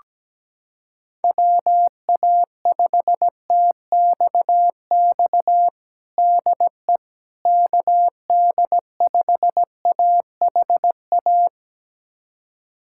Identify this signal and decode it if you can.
Morse code, heard as a keyed tone, about 17 wpm, 700 Hz. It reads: WA5TXX DE KD5AHA
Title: USA Callsign DE Callsign - 6 Characters - 17wpm